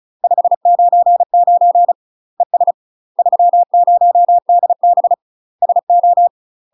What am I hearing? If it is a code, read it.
Morse code, heard as a keyed tone, about 35 wpm, 700 Hz. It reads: 599 ES 30DB SO
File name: UK MASTER QSOs TRAINING / 35WPM / UK MASTER QSOs 0016